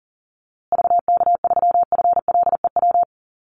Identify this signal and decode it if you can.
Morse code, heard as a keyed tone, about 40 wpm, 700 Hz. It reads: VK3FLEW
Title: USA Call Signs Repeated 3x 40wpm